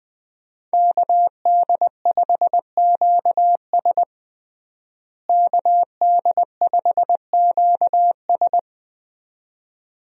Morse code, heard as a keyed tone, about 20 wpm, 700 Hz. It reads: KD5QS KD5QS
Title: USA Call Signs - 5 Characters - Repeated 2x - 20wpm